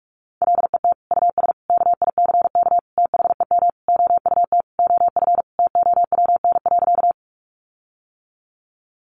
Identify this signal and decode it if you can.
Morse code, heard as a keyed tone, about 44 wpm, 700 Hz. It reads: LET US KICK THEM OUT OF TOWN.
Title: Sentences from Top 1300 Words 45wpm